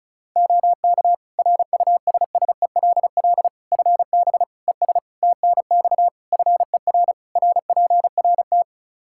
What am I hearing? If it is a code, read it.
Morse code, heard as a keyed tone, about 36 wpm, 700 Hz. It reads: OK RUSSELL FB ES TNX FER RPRT
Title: Practice QSO - Exchange 2 - Info - 35wpm